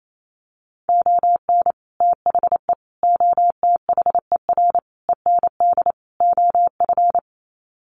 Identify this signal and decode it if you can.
Morse code, heard as a keyed tone, about 28 wpm, 700 Hz. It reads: ON THE OTHER END OF
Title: Sets of 5 Words Encore 28wpm